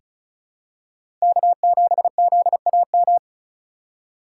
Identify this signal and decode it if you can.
Morse code, heard as a keyed tone, about 35 wpm, 690 Hz. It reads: K7ZAM
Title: USA Call Signs - 5 Characters - 2x Character Spacing - 35wpm